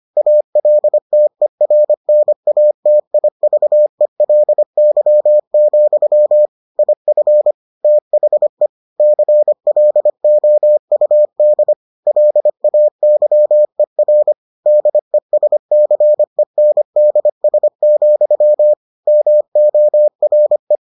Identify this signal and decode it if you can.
Morse code, heard as a keyed tone, about 25 wpm, 600 Hz. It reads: ALTERNATIVELY, IF THE CLOUD LAYER DESCENDS, MORE